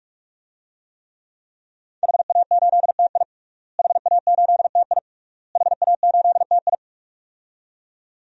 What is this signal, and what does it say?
Morse code, heard as a keyed tone, about 46 wpm, 700 Hz. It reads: HA8TI HA8TI HA8TI
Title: DXCC-100 Call Signs - 5 Characters - Repeated 3x - 45wpm